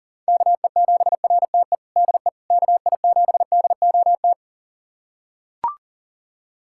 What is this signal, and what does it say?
Morse code, heard as a keyed tone, about 40 wpm, 700 Hz. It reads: KE7RTE DE KI7DOT
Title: USA Callsign DE Callsign - 6 Characters - 40wpm